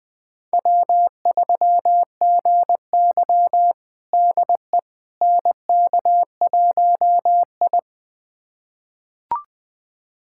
Morse code, heard as a keyed tone, about 20 wpm, 700 Hz. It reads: W3GY DE NK1I
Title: USA Callsign DE Callsign - 4 Characters - 20wpm